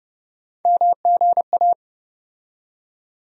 Morse code, heard as a keyed tone, about 31 wpm, 700 Hz. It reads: MGA
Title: USA Call Sign Postfixes - 3 Letters 30wpm